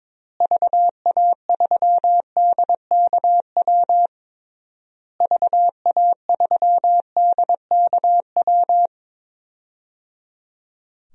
Morse code, heard as a keyed tone, about 22 wpm, 700 Hz. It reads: VA3DKW VA3DKW
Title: DXCC-100 Call Signs - 6 Characters - Repeated 3x - 22wpm